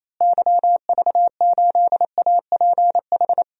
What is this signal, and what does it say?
Morse code, heard as a keyed tone, about 28 wpm, 700 Hz. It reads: YV8APH
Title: DXCC-100 Call Signs - 6 Characters - Repeated 3x - 28wpm